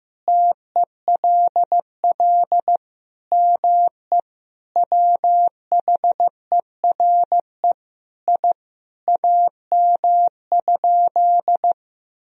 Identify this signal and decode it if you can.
Morse code, heard as a keyed tone, about 15 wpm, 700 Hz. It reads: TELL ME WHERE I AM?
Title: Sentences from Top 1000 Words - Review Part 3 - 15wpm